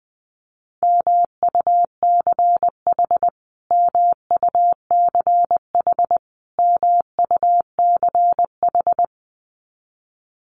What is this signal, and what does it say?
Morse code, heard as a keyed tone, about 20 wpm, 700 Hz. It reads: MUCH MUCH MUCH
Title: Top 200 Words Repeated 20wpm